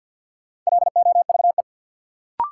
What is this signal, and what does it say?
Morse code, heard as a keyed tone, about 52 wpm, 700 Hz. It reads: LOVE
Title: Sets of 5 Words Encore - New Word Drill - 50wpm